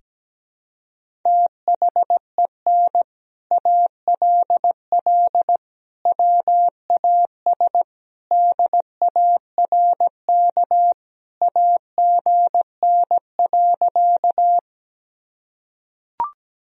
Morse code, heard as a keyed tone, about 17 wpm, 700 Hz. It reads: THEN ALL WAS DARK AGN.
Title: Sentences from Top 500 Words and Abbreviations 17wpm